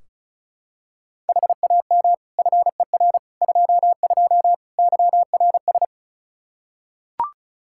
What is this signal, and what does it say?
Morse code, heard as a keyed tone, about 36 wpm, 700 Hz. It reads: HAM FER 22 YRS
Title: Practice QSO - Exchange 3 - More Info - 35wpm